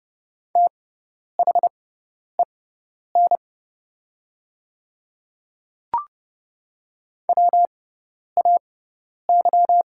Morse code, heard as a keyed tone, about 30 wpm, spaced as for 10 wpm, 700 Hz. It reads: THEN WAY
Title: 6x Character Spacing - Top 100 Words - 30wpm